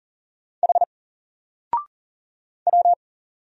Morse code, heard as a keyed tone, about 40 wpm, 700 Hz. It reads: H W
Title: Single Letter-Number 40wpm